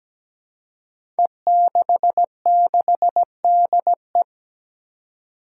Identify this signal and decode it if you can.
Morse code, heard as a keyed tone, about 17 wpm, 700 Hz. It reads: E66DE
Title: Five Challenging Letter-Number Combinations 17wpm